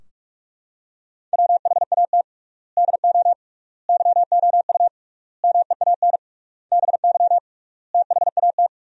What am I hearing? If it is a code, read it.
Morse code, heard as a keyed tone, about 44 wpm, 700 Hz. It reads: WHAT DO YOU MEAN BY THAT
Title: 3x Word-Spacing - Sentences from Top 400 Words - 45wpm